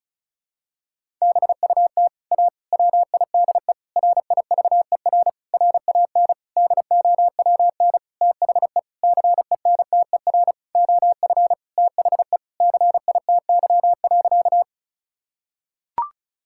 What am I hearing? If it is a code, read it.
Morse code, heard as a keyed tone, about 36 wpm, 700 Hz. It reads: BUT A WIDE RIVER RAN DOWN THE CENTER OF THE CITY.
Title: Sentences from Top 1000 Words - Review Part 6 - 35wpm